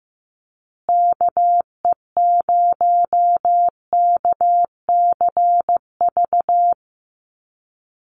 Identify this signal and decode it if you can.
Morse code, heard as a keyed tone, about 15 wpm, 700 Hz. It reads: KE0KCV
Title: USA Call Signs Repeated 3x 15wpm